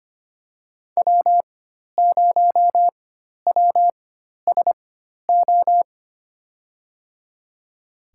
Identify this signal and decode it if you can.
Morse code, heard as a keyed tone, about 25 wpm, spaced as for 12 wpm, 700 Hz. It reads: W0WSO
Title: USA Call Signs - 5 Characters - 4x Character Spacing - 25wpm